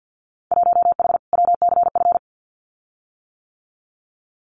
Morse code, heard as a keyed tone, about 50 wpm, 700 Hz. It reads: 1H WCF
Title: Winter Field Day Exchanges Encore 50wpm